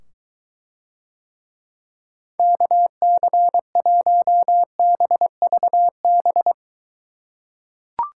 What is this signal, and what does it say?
Morse code, heard as a keyed tone, about 23 wpm, 700 Hz. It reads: KC1BVB